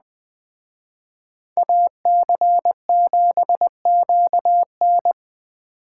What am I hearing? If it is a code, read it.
Morse code, heard as a keyed tone, about 21 wpm, 700 Hz. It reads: AC7QN